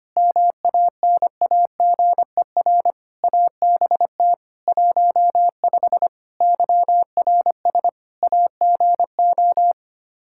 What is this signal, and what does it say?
Morse code, heard as a keyed tone, about 25 wpm, 700 Hz. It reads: MANAGER ABT 15 YRS AGO
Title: Practice QSO - Exchange 3 - More Info - 25wpm